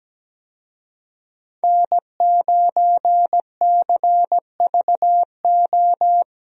Morse code, heard as a keyed tone, about 17 wpm, 700 Hz. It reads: N9CVO
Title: USA Call Signs - 5 Characters - Repeated 3x - 17wpm